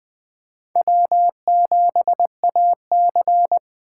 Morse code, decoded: W7AC

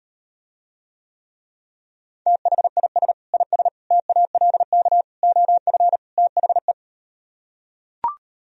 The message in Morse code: THIS IS TALK OF THE

